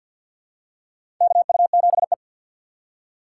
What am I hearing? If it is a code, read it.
Morse code, heard as a keyed tone, about 49 wpm, 690 Hz. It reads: KU7E
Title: USA Call Signs - 4 Characters - 6x Character Spacing - 50wpm